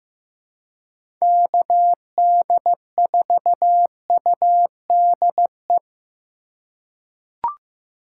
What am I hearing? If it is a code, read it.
Morse code, heard as a keyed tone, about 15 wpm, 700 Hz. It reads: KD4UDE
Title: USA Call Signs - 3x Character Spacing - 15wpm